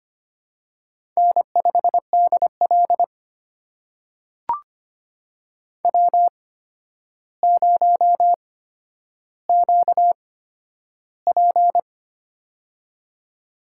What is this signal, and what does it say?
Morse code, heard as a keyed tone, about 25 wpm, spaced as for 11 wpm, 700 Hz. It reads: N5DL W0QP
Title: USA Call Signs - 4 Characters - 8x Character Spacing - 25wpm